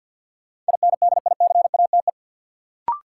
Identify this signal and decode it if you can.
Morse code, heard as a keyed tone, about 46 wpm, 700 Hz. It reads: INDICATE
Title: Sentences from Top 1300 Words - New Word Drill Alt - 50wpm